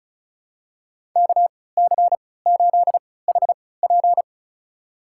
Morse code, heard as a keyed tone, about 36 wpm, spaced as for 20 wpm, 700 Hz. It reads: KC8HP
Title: USA Call Signs - 5 Characters - 3x Character Spacing - 35wpm